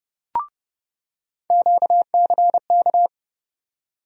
Morse code, heard as a keyed tone, about 30 wpm, 700 Hz. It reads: QCK
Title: USA Call Sign Postfixes - 3 Letters 30wpm